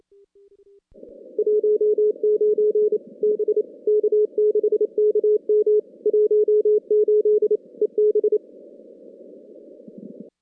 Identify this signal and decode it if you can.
Morse code, heard as a keyed tone, about 27 wpm, 410 Hz. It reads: 19 B K6KM 18 EB
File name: contests / CW-SS-k5tr- -2100